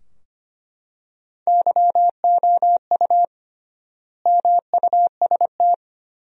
Morse code, heard as a keyed tone, about 25 wpm, 700 Hz. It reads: YOU MUST
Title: 3x Word-Spacing - Sentences from Top 400 Words - 25wpm